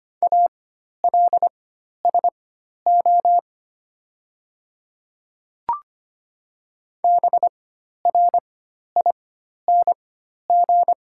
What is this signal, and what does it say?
Morse code, heard as a keyed tone, about 25 wpm, spaced as for 12 wpm, 700 Hz. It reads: ALSO BRING
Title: 4x Character Spacing - Top 500 Words - 25wpm